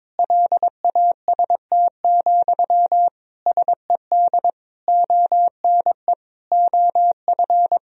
Morse code, heard as a keyed tone, about 22 wpm, 700 Hz. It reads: LAST, SED ONE OF
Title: Sentences from Top 300 Words and Abbreviations 22wpm